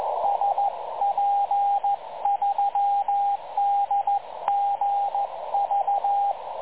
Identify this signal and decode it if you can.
Morse code, heard as a keyed tone, about 14 wpm, 810 Hz, with radio noise. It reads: SP3DGV
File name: dlazaawansowanych / nagraniazpasma / websdr recording 3565.0kHz